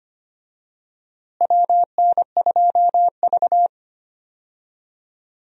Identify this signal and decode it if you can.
Morse code, heard as a keyed tone, about 25 wpm, 700 Hz. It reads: WN2V
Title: USA Call Signs - 4 Characters - Repeated 3x - 25wpm